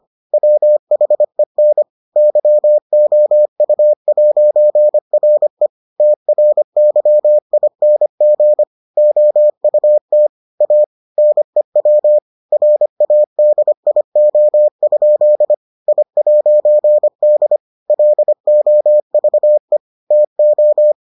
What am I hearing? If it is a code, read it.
Morse code, heard as a keyed tone, about 25 wpm, 600 Hz. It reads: WHEN YOU'RE TRYING OUT A NEW RADIO? I'D LOVE TO